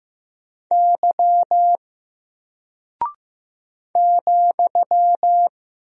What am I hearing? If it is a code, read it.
Morse code, heard as a keyed tone, about 15 wpm, 700 Hz. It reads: Y ,
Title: Single Character (Letter:Number:Punctuation:Prosign) Alt 15wpm